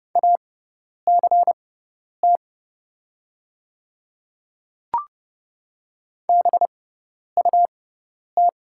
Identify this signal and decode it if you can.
Morse code, heard as a keyed tone, about 30 wpm, spaced as for 10 wpm, 700 Hz. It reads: ACT BUT
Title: 6x Character Spacing - Top 500 Words - 30wpm